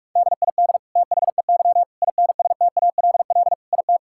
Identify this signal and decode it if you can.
Morse code, heard as a keyed tone, about 42 wpm, 700 Hz. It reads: DID THEY INSTALL IT
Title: Sentences from Top 1600 Words 45wpm